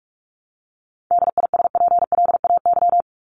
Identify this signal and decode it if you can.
Morse code, heard as a keyed tone, about 44 wpm, 700 Hz. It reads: DISPLAY
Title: Top 1001-1250 Words 45wpm